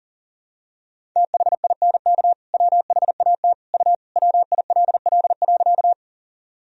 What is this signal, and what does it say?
Morse code, heard as a keyed tone, about 37 wpm, 700 Hz. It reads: THINK WHAT U WILL.
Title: Sentences from Top 200 Words and Abbreviations 40wpm